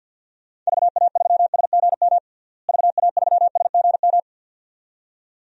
Morse code, heard as a keyed tone, about 50 wpm, 700 Hz. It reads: VA3SGM VA3SGM
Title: DXCC-100 Call Signs - 6 Characters - Repeated 2x - 50wpm